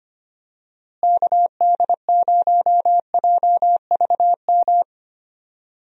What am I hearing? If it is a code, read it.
Morse code, heard as a keyed tone, about 25 wpm, 700 Hz. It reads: KD0JVM
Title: USA Call Signs - 6 Characters - Repeated 2x - 25wpm